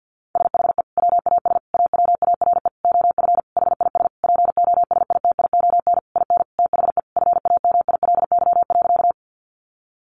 Morse code, heard as a keyed tone, about 50 wpm, 700 Hz. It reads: SHE WAS AWARE OF HIS POSITION IN THE FAMILY.
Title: Sentences from Top 1200 Words 50wpm